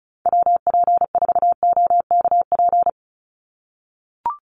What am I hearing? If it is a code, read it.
Morse code, heard as a keyed tone, about 34 wpm, 700 Hz. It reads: WP4OKP
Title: USA Call Signs 35wpm